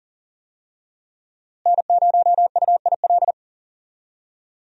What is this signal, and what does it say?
Morse code, heard as a keyed tone, about 40 wpm, 700 Hz. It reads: N0UIL
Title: USA Call Signs - 5 Characters - Repeated 2x - 40wpm